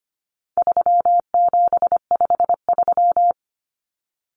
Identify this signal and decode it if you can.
Morse code, heard as a keyed tone, about 25 wpm, 700 Hz. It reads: 3753
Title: Common Cut Numbers - 4-Digit 25wpm